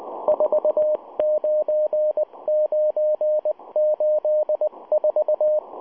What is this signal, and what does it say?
Morse code, heard as a keyed tone, about 19 wpm, 610 Hz, with radio noise. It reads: 49984